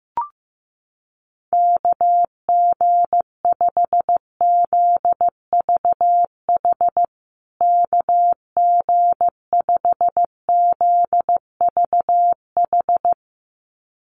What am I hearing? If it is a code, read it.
Morse code, heard as a keyed tone, about 15 wpm, 700 Hz. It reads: KG5ZVH KG5ZVH
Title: USA Call Signs Repeated 2x 15wpm